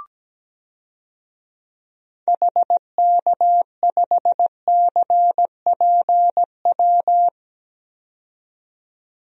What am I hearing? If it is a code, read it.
Morse code, heard as a keyed tone, about 17 wpm, 700 Hz. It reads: HK5CPW
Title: DXCC-100 Call Signs - 6 Characters - 17wpm